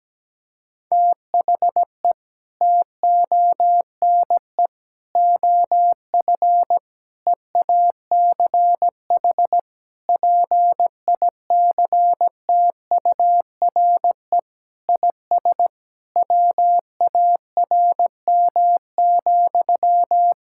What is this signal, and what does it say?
Morse code, heard as a keyed tone, about 17 wpm, 700 Hz. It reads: THE TONE OF EACH PICTURE IS WARM,
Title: Sentences from Top 1600 Words 17wpm